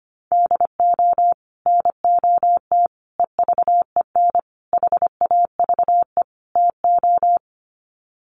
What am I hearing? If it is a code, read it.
Morse code, heard as a keyed tone, about 25 wpm, 700 Hz. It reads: DO NOT EVEN HAVE TO
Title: Sets of 5 Words Encore2 25wpm